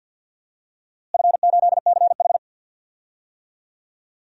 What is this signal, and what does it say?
Morse code, heard as a keyed tone, about 52 wpm, 700 Hz. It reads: F8CH
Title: DXCC-100 Call Signs - 4 Characters - Repeated 3x - 50wpm